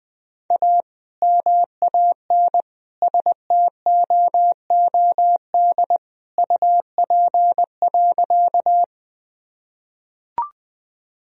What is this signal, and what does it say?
Morse code, heard as a keyed tone, about 20 wpm, 700 Hz. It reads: A MAN STOOD UP.
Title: Sentences from Top 1000 Words - Review Part 2 - 20wpm